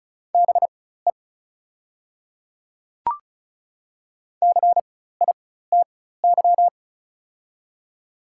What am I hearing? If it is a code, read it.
Morse code, heard as a keyed tone, about 36 wpm, spaced as for 16 wpm, 700 Hz. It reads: BE CITY
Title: 4x Character Spacing - Top 300 Words - 35wpm